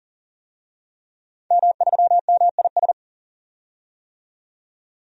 Morse code, heard as a keyed tone, about 40 wpm, 700 Hz. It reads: M3MIS
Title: DXCC-100 Call Signs - 5 Characters - 40wpm